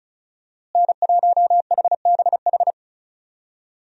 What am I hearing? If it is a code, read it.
Morse code, heard as a keyed tone, about 34 wpm, 700 Hz. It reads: N1HBH
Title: USA Call Signs - 5 Characters - Repeated 2x - 35wpm